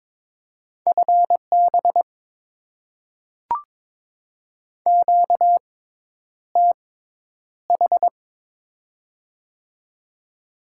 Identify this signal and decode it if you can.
Morse code, heard as a keyed tone, about 22 wpm, spaced as for 10 wpm, 700 Hz. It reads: FB QTH
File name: Top 20 QSO Elements - 6x Character Spacing - 22wpm